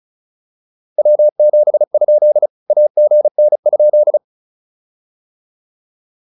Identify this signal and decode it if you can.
Morse code, heard as a keyed tone, about 36 wpm, 600 Hz. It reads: W7? AGN?